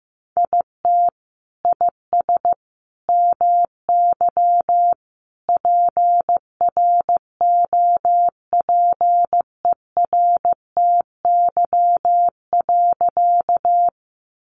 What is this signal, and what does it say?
Morse code, heard as a keyed tone, about 15 wpm, 700 Hz. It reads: IT IS MY PROPERTY.
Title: Sentences from Top 1000 Words 15wpm